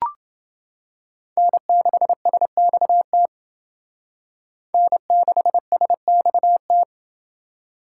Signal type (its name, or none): Morse code